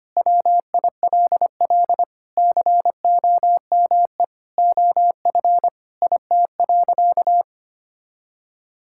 Morse code, heard as a keyed tone, about 25 wpm, 700 Hz. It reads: WILL COME OF IT.
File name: practice / 25wpm / Sentences with 2-Word Phrases - 25wpm